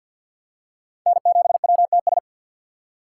Morse code, heard as a keyed tone, about 51 wpm, 700 Hz. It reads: N7WTS
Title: USA Call Signs - 5 Characters - 3x Character Spacing - 50wpm